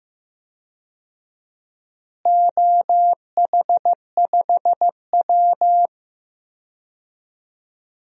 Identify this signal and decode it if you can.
Morse code, heard as a keyed tone, about 15 wpm, 700 Hz. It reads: OH5W
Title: DXCC-100 Call Signs - 4 Characters - Repeated 3x - 15wpm